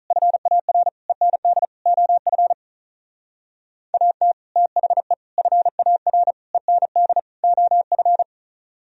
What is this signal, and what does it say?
Morse code, heard as a keyed tone, about 45 wpm, 700 Hz. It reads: FAR END OF AT THE FAR END OF